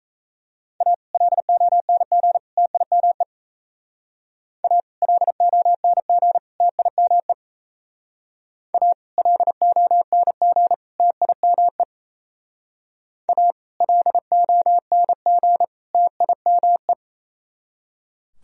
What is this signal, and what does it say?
Morse code, heard as a keyed tone, about 34 wpm, 700 Hz. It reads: A LONG TIME A LONG TIME A LONG TIME A LONG TIME
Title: Speed-Racing - Sets of 3 Words 28wpm